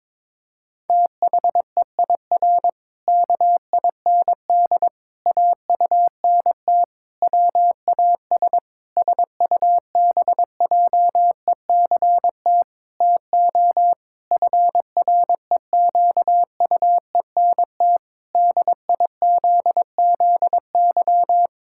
Morse code, heard as a keyed tone, about 22 wpm, 700 Hz. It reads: THEIR KIND AUNT WAS SUBJECT TO FREQUENT DIZZY
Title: Pangram Challenge v2 - 22wpm